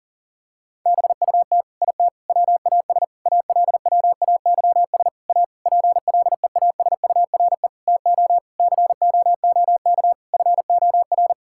Morse code, heard as a keyed tone, about 37 wpm, 700 Hz. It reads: BUT IT WAS ALWAYS A PLEASURE TO COOK FOR
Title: Sentences from Top 1600 Words 40wpm